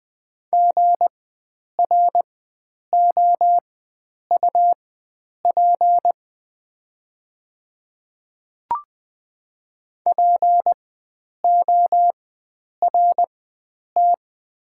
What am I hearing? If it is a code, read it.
Morse code, heard as a keyed tone, about 20 wpm, spaced as for 9 wpm, 700 Hz. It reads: GROUP PORT
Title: 4x Character Spacing - Top 300 Words - 20wpm